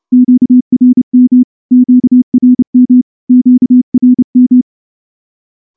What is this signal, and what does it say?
Morse code, heard as a keyed tone, about 29 wpm, 250 Hz. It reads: QRM QRM QRM
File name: morse-training-30WPM